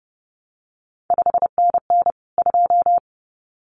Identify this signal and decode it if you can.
Morse code, heard as a keyed tone, about 30 wpm, 700 Hz. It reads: 5NN 2
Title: CQ WW DX Contest Exchange Encore - 30wpm